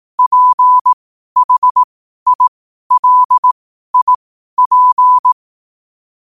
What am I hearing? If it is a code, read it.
Morse code, heard as a keyed tone, about 18 wpm, spaced as for 12 wpm, 1000 Hz. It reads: PHILIP
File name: qotdfiles / 12wpm